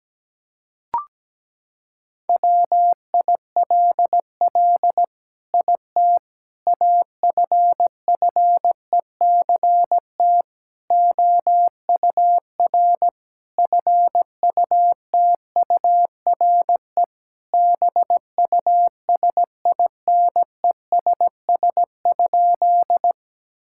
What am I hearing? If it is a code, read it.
Morse code, heard as a keyed tone, about 17 wpm, 700 Hz. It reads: WILL IT AFFECT OUR FUTURE BUSINESS?
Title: Sentences from Top 1000 Words - Review Part 6 - 17wpm